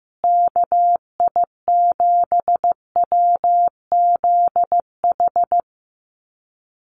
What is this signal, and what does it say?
Morse code, heard as a keyed tone, about 15 wpm, 700 Hz. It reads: KI7WZH
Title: USA Call Signs Repeated 2x 15wpm